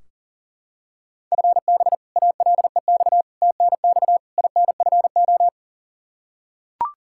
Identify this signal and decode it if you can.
Morse code, heard as a keyed tone, about 40 wpm, 700 Hz. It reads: FB ALEX TNX INFO
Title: Practice QSO - Exchange 3 - More Info - 40wpm